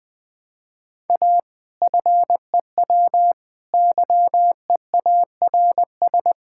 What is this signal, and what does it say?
Morse code, heard as a keyed tone, about 21 wpm, 700 Hz. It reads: A FEW YEARS
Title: Sentences with 4-Word Phrases - Spoken Beginning - 20wpm